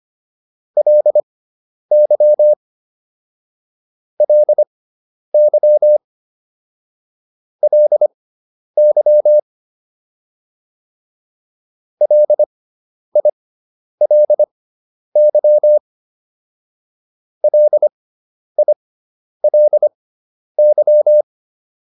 Morse code, heard as a keyed tone, about 25 wpm, spaced as for 10 wpm, 600 Hz. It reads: LY LY LY LILY LILY